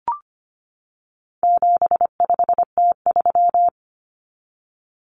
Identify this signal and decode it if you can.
Morse code, heard as a keyed tone, about 25 wpm, 700 Hz. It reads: 75T3
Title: Common Cut Numbers - 4-Digit 25wpm